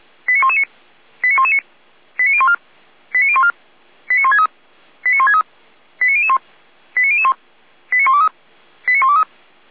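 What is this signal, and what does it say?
Five-tone selective call (ZVEI), heard as quick runs of five tones, tones of about 70 ms. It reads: 89119 89119 89014 89014 89163 89163 89001 89001 89123 89123
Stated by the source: Five-tone ZVEI selective call 89010 received on an SDR, recorded by Feuerwehr Orlamünde.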